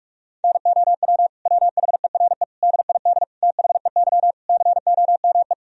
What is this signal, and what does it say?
Morse code, heard as a keyed tone, about 41 wpm, 700 Hz. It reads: NOW WHERE DID THEY COME
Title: Sentences from Top 200 Words and Abbreviations 45wpm